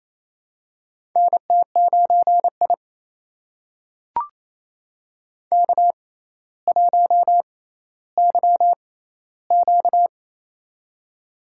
Morse code, heard as a keyed tone, about 27 wpm, spaced as for 14 wpm, 700 Hz. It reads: NT9I K1YQ
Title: USA Call Signs - 4 Characters - 6x Character Spacing - 28wpm